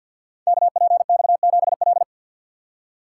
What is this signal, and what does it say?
Morse code, heard as a keyed tone, about 48 wpm, 700 Hz. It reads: KPX7L
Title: Code Groups - 5 Letter:Numbers - 50wpm